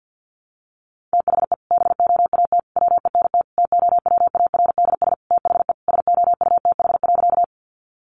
Morse code, heard as a keyed tone, about 50 wpm, 700 Hz. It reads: THE BOAT WENT TOWARDS THE SOUTH.
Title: Sentences from Top 1100 Words 50wpm